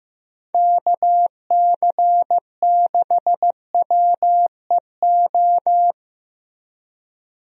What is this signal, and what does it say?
Morse code, heard as a keyed tone, about 15 wpm, 700 Hz. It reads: KC6WEO
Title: USA Call Signs - 6 Characters - Repeated 2x - 15wpm